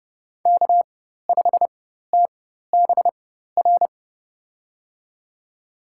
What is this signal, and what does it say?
Morse code, heard as a keyed tone, about 30 wpm, spaced as for 14 wpm, 700 Hz. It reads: K5TBR
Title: USA Call Signs - 5 Characters - 4x Character Spacing - 30wpm